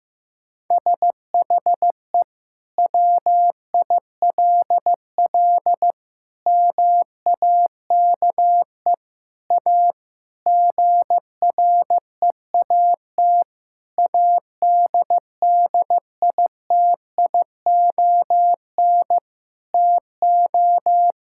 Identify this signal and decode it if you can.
Morse code, heard as a keyed tone, about 15 wpm, 700 Hz. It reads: SHE WILL MAKE A GREAT ADDITION TO
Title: Sentences from Top 1000 Words - Review Part 6 - 15wpm